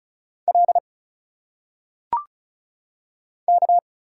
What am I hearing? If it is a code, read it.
Morse code, heard as a keyed tone, about 34 wpm, 700 Hz. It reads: L K
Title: Single Letters - Phonetic Pronunciation 35wpm